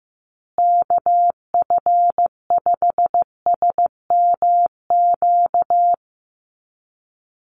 Morse code, heard as a keyed tone, about 15 wpm, 700 Hz. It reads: KF5SMQ
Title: USA Call Signs Repeated 3x 15wpm